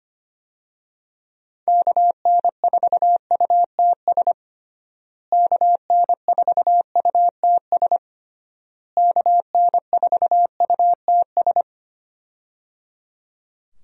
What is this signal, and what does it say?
Morse code, heard as a keyed tone, about 25 wpm, 700 Hz. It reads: KN4UTS KN4UTS KN4UTS
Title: USA Call Signs - 6 Characters - Repeated 3x - 25wpm